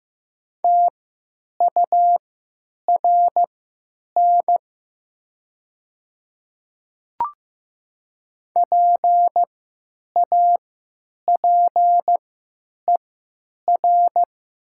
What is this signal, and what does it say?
Morse code, heard as a keyed tone, about 15 wpm, spaced as for 9 wpm, 700 Hz. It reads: TURN PAPER
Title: 3x Character Spacing - Top 300 Words - 15wpm